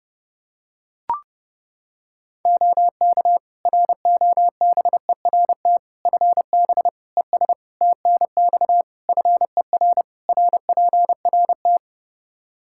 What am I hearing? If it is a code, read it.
Morse code, heard as a keyed tone, about 30 wpm, 700 Hz. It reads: OK ROBERT FB ES TNX FER RPRT
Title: K1USN SST QSO Practice 30wpm